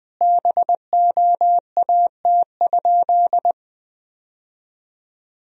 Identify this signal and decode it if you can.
Morse code, heard as a keyed tone, about 20 wpm, 700 Hz. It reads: BOAT?
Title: Sentences from Top 1000 Words - Review Part 4 - 20wpm